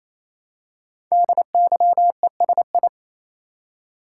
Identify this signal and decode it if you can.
Morse code, heard as a keyed tone, about 27 wpm, 700 Hz. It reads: DYESI
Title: Code Groups - 5 Letters - 28wpm